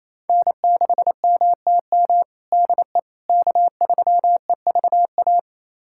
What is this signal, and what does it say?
Morse code, heard as a keyed tone, about 28 wpm, 700 Hz. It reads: N6MTM DE K3EVA
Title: USA Callsign DE Callsign - 5 Characters - 28wpm